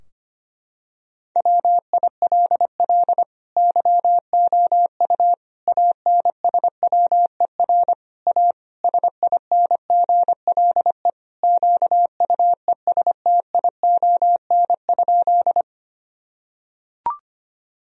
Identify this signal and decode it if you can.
Morse code, heard as a keyed tone, about 25 wpm, 700 Hz. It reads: WILL YOU ANSWER A SINGLE QUESTION?
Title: Sentences from Top 1000 Words - Review Part 4 - 25wpm